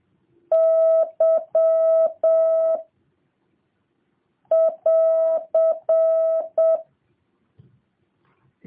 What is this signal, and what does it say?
Morse code, heard as a keyed tone, about 7 wpm, 650 Hz. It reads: Y <AR>